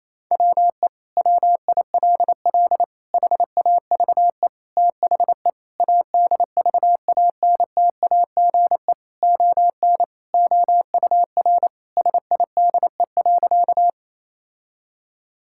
Morse code, heard as a keyed tone, about 28 wpm, 700 Hz. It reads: WE WILL HAVE THE ADVANTAGE ON OUR SIDE.
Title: Sentences from Top 1000 Words - Review Part 6 - 28wpm